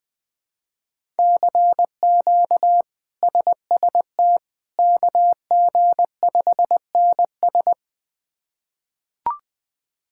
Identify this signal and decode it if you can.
Morse code, heard as a keyed tone, about 20 wpm, 700 Hz. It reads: CQ SST KG5NS
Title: K1USN SST QSO Practice - 20wpm